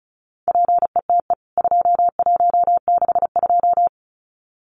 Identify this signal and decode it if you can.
Morse code, heard as a keyed tone, about 34 wpm, 700 Hz. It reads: PETE 2162
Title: CWT Exchange 35wpm